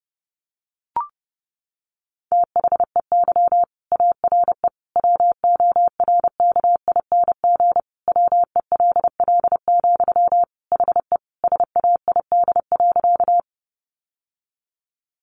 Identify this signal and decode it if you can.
Morse code, heard as a keyed tone, about 30 wpm, 700 Hz. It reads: THEY ARE WORKING WELL, HE SAID.
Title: Sentences from Top 700 Words 30wpm